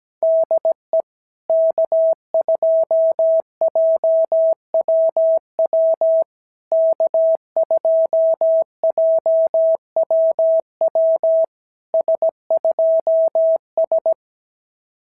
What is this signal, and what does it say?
Morse code, heard as a keyed tone, about 17 wpm, 650 Hz. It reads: DE K2JWW K2JWW S2S
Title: SOTA QSO Practice 17wpm